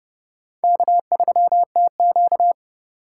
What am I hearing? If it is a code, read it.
Morse code, heard as a keyed tone, about 30 wpm, 700 Hz. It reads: K3TQ
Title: USA Call Signs - 4 Characters - 2x Character Spacing - 30wpm